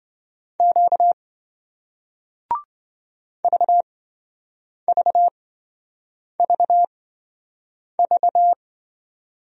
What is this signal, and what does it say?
Morse code, heard as a keyed tone, about 28 wpm, spaced as for 7 wpm, 700 Hz. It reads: Q VVVV